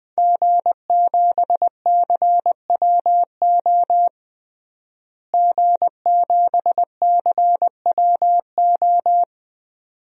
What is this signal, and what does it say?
Morse code, heard as a keyed tone, about 21 wpm, 700 Hz. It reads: G7CWO G7CWO
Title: DXCC-100 Call Signs - 5 Characters - Repeated 2x - 20wpm